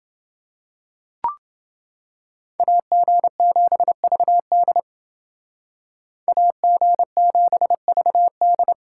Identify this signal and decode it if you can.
Morse code, heard as a keyed tone, about 29 wpm, 700 Hz. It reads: AG7VD AG7VD